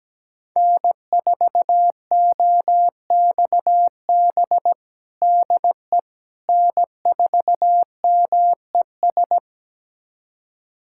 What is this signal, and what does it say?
Morse code, heard as a keyed tone, about 17 wpm, 700 Hz. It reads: N4OXB DE N4MES
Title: USA Callsign DE Callsign - 5 Characters - 17wpm